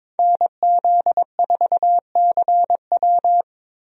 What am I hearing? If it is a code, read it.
Morse code, heard as a keyed tone, about 22 wpm, 700 Hz. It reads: NZ4CW
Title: USA Call Signs - 5 Characters - Repeated 2x - 22wpm